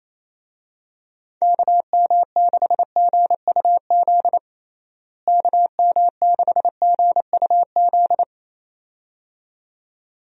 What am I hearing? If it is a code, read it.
Morse code, heard as a keyed tone, about 28 wpm, 700 Hz. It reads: KM6GUZ KM6GUZ
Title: USA Call Signs - 6 Characters - Repeated 2x - 28wpm